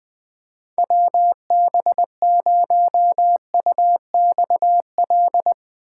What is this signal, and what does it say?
Morse code, heard as a keyed tone, about 20 wpm, 700 Hz. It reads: WB0UXL
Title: Speed Racing - USA 2x3 Call Signs 17wpm